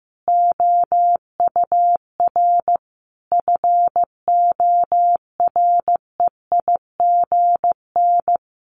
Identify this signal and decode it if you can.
Morse code, heard as a keyed tone, about 15 wpm, 700 Hz. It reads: OUR FOREIGN
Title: Sentences from Top 1400 Words 15wpm